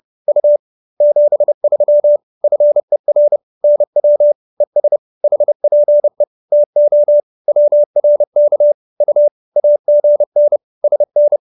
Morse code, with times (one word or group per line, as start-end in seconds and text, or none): 0.3-0.6 U
1.0-2.2 73
2.4-3.4 FER
3.6-4.3 NW
4.6-5.0 ES
5.2-6.2 HPE
6.5-7.2 TO
7.5-8.7 WRK
9.0-9.3 U
9.6-10.6 AGN
10.8-11.4 SN